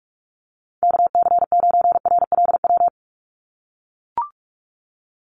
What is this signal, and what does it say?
Morse code, heard as a keyed tone, about 44 wpm, 700 Hz. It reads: KC9RLW
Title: USA Call Signs 45wpm